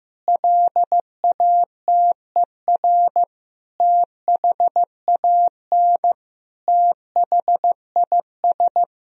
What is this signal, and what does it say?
Morse code, heard as a keyed tone, about 16 wpm, 700 Hz. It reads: LATER THAN THIS
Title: Sentences from Top 1000 Words - Review Part 4 - 15wpm